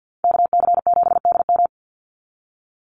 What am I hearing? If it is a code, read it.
Morse code, heard as a keyed tone, about 50 wpm, 700 Hz. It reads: KC7DM
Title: NA SOTA Activator Callsigns 50wpm